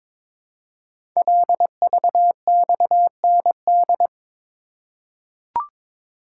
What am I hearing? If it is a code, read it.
Morse code, heard as a keyed tone, about 22 wpm, 700 Hz. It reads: LVXND
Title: Code Groups - 5 Letters - 22wpm